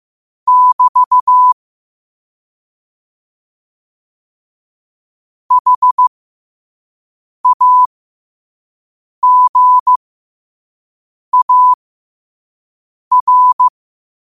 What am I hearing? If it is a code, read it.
Morse code, heard as a keyed tone, about 15 wpm, spaced as for 5 wpm, 1000 Hz. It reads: <BT> HAGAR
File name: qotdfiles / 5wpm